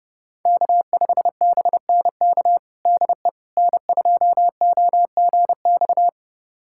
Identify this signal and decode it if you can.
Morse code, heard as a keyed tone, about 30 wpm, 700 Hz. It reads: K5BNK DE N2OGX
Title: USA Callsign DE Callsign - 5 Characters - 30wpm